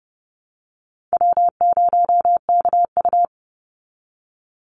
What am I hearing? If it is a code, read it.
Morse code, heard as a keyed tone, about 30 wpm, 700 Hz. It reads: W0KU
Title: Easy Call Signs Repeated 3x - 30wpm